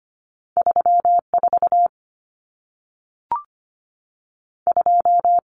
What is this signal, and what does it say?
Morse code, heard as a keyed tone, about 25 wpm, 700 Hz. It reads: 34 2
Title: Numbers - 2-Digit 25wpm